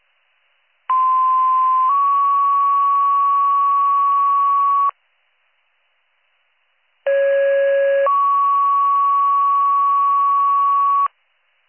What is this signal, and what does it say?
Two-tone page, heard as a one-second tone, then a three-second tone, about 1040 then 1120 Hz, about 550 then 1120 Hz. It is heard from 0.9 to 11.1 s.